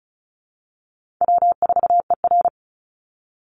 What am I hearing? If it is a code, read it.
Morse code, heard as a keyed tone, about 34 wpm, 700 Hz. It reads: W4ER
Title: Call Signs Repeated 3x - 35wpm